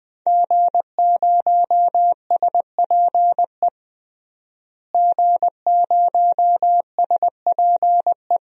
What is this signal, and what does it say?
Morse code, heard as a keyed tone, about 20 wpm, 700 Hz. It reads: G0SPE G0SPE
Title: DXCC-100 Call Signs - 5 Characters - Repeated 3x - 20wpm